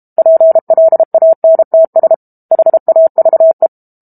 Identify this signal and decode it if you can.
Morse code, heard as a keyed tone, about 32 wpm, 650 Hz. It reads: PLANTS HAVE
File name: Book Two / 06-Chapter 6 - 32 WPM